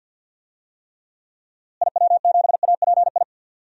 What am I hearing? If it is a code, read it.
Morse code, heard as a keyed tone, about 53 wpm, 700 Hz. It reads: IW7API
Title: DXCC-100 Call Signs - 6 Characters - Repeated 3x - 50wpm